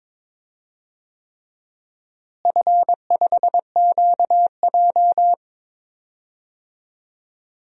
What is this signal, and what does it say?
Morse code, heard as a keyed tone, about 22 wpm, 700 Hz. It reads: F5QJ
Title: DXCC-100 Call Signs - 4 Characters - Repeated 3x - 22wpm